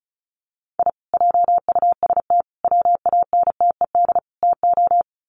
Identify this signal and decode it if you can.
Morse code, heard as a keyed tone, about 34 wpm, 700 Hz. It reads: I JUST WANTED TO
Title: Sets of 4 Words 35wpm